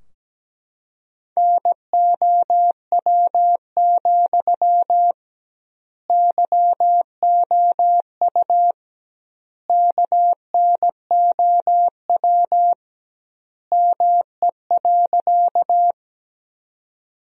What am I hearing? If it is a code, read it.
Morse code, heard as a keyed tone, about 17 wpm, 700 Hz. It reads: NOW, YOU KNOW ME.
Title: 2x Word-Spacing - Sentences from Top 200 Words - 17wpm